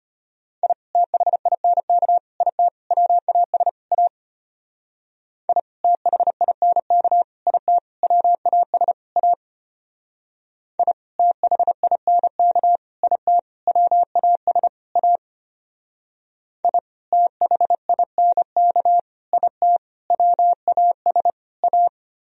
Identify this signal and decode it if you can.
Morse code, heard as a keyed tone, about 32 wpm, 700 Hz. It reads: I THINK IT WAS A I THINK IT WAS A I THINK IT WAS A I THINK IT WAS A